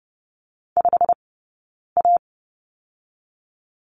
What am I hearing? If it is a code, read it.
Morse code, heard as a keyed tone, about 30 wpm, 700 Hz. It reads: 5 A
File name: course / 30:10wpm / 086 Lesson 15d - Number 5 - Two Characters 30wpm